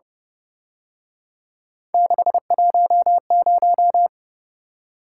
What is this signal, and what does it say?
Morse code, heard as a keyed tone, about 29 wpm, 700 Hz. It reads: 610
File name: Speed-Racing - Three Letter-Number Combinations - 25wpm